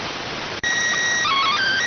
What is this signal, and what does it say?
SSTV (slow-scan television), PD 290.